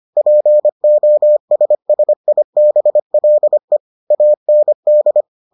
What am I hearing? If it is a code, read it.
Morse code, heard as a keyed tone, about 25 wpm, 600 Hz. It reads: POSSIBLE AND